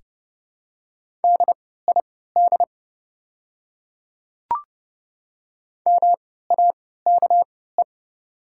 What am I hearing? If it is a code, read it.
Morse code, heard as a keyed tone, about 29 wpm, spaced as for 17 wpm, 700 Hz. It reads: DID MAKE
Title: 3x Character Spacing - Top 200 Words - 30wpm